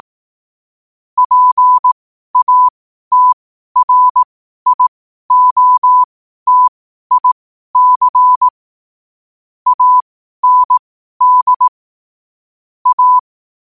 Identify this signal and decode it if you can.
Morse code, heard as a keyed tone, about 18 wpm, spaced as for 12 wpm, 1000 Hz. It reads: PATRIOTIC AND A
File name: qotdfiles / 12wpm